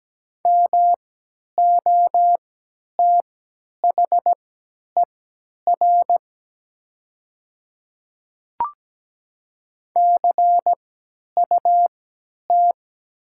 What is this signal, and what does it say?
Morse code, heard as a keyed tone, about 17 wpm, spaced as for 10 wpm, 700 Hz. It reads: MOTHER CUT
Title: 3x Character Spacing - Top 300 Words - 17wpm